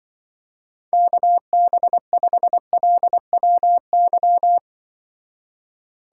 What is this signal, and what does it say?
Morse code, heard as a keyed tone, about 24 wpm, 700 Hz. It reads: KB5LWY
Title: Speed Racing - USA Call Signs 20wpm